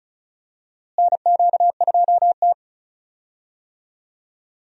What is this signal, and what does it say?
Morse code, heard as a keyed tone, about 36 wpm, 700 Hz. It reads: NQ2T
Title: USA Call Signs - 4 Characters - Repeated 3x - 35wpm